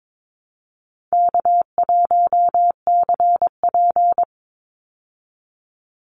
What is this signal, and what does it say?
Morse code, heard as a keyed tone, about 22 wpm, 700 Hz. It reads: K1CP
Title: Call Signs Repeated 2x - 22wpm